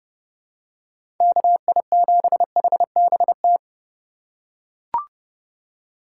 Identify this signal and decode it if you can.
Morse code, heard as a keyed tone, about 31 wpm, 700 Hz. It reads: KI7HBT
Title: USA Call Signs - 6 Characters - 3x Character Spacing - 30wpm